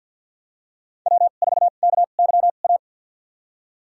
Morse code, heard as a keyed tone, about 53 wpm, spaced as for 37 wpm, 700 Hz. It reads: W4KYA
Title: USA Call Signs - 5 Characters - 2x Character Spacing - 50wpm